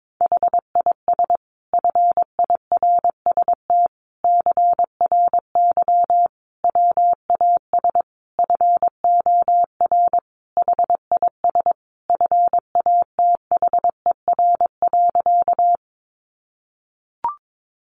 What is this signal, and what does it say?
Morse code, heard as a keyed tone, about 22 wpm, 700 Hz. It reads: HIS FIRST CRY WAS FOR HIS FATHER.
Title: Sentences from Top 500 Words 22wpm